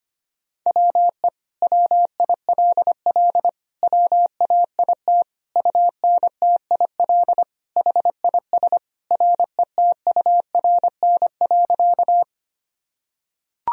Morse code, heard as a keyed tone, about 25 wpm, 700 Hz. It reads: WE WILL WAIT UNTIL HIS RETURN.
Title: Sentences from Top 1000 Words - Review Part 1 - 25wpm